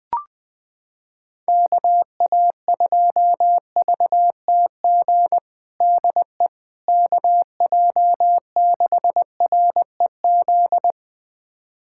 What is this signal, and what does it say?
Morse code, heard as a keyed tone, about 20 wpm, 700 Hz. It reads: KA2VTG DE KJ6REZ
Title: USA Callsign DE Callsign - 6 Characters - 20wpm